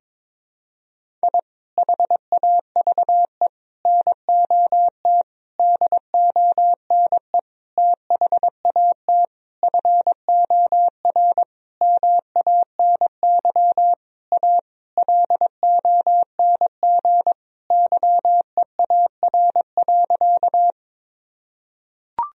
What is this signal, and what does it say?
Morse code, heard as a keyed tone, about 22 wpm, 700 Hz. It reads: I HAVE NOT DONE THAT FOR MANY A LONG YEAR.
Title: Sentences from Top 1000 Words - Review Part 1 - 22wpm